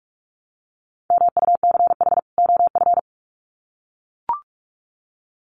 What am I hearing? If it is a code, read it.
Morse code, heard as a keyed tone, about 44 wpm, 700 Hz. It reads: MUCH OF
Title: Sets of 2 Words 45wpm